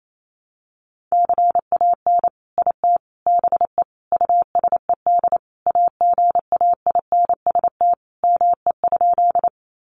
Morse code, heard as a keyed tone, about 28 wpm, 700 Hz. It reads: CAN IT BE USED AGAINST ME?
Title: Sentences from Top 600 Words 28wpm